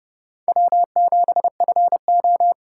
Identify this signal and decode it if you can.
Morse code, heard as a keyed tone, about 29 wpm, 700 Hz. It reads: W7FO